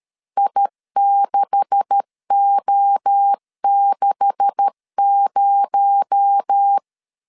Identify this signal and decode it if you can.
Morse code, heard as a keyed tone, about 13 wpm, 800 Hz. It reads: I6O60
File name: LEZIONI / 9 286 12wpm 2305